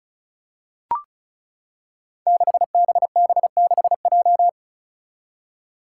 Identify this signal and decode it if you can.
Morse code, heard as a keyed tone, about 36 wpm, 700 Hz. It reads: 6BB6J
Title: Five Challenging Letter-Number Combinations 35wpm